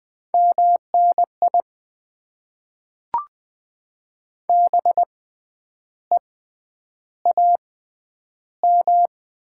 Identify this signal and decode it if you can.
Morse code, heard as a keyed tone, about 20 wpm, spaced as for 9 wpm, 700 Hz. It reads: MNI BEAM
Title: Top 90 QSO Elements - 6x Character Spacing - 20wpm